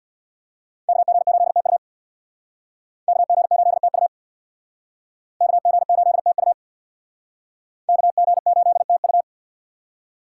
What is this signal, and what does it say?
Morse code, heard as a keyed tone, about 65 wpm, 700 Hz. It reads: KG8TU KG8TU KG8TU KG8TU